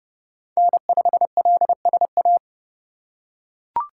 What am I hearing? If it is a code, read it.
Morse code, heard as a keyed tone, about 30 wpm, 700 Hz. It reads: N5LSA
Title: USA Call Signs - 5 Characters - 3x Character Spacing - 30wpm